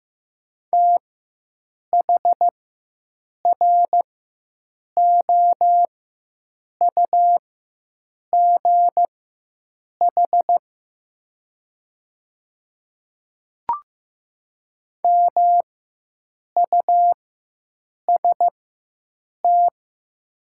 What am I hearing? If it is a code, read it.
Morse code, heard as a keyed tone, about 15 wpm, spaced as for 7 wpm, 700 Hz. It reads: THROUGH MUST
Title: 4x Character Spacing - Top 200 Words - 15wpm